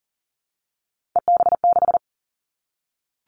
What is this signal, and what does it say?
Morse code, heard as a keyed tone, about 40 wpm, 700 Hz. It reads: EB6
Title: Callsign Prefixes for Top 50 DXCC - 40wpm